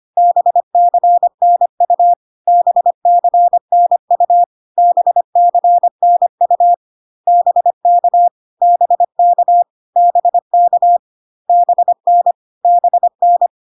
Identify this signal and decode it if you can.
Morse code, heard as a keyed tone, about 25 wpm, 700 Hz. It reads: BCNU BCNU BCNU BK BK BK BN BN
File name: CW abbreviations alphabetical repeated 3 times 25 25wpm